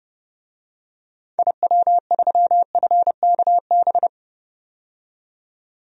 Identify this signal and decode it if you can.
Morse code, heard as a keyed tone, about 32 wpm, 700 Hz. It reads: IW3FKB
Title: DXCC-100 Call Signs - 6 Characters - 30wpm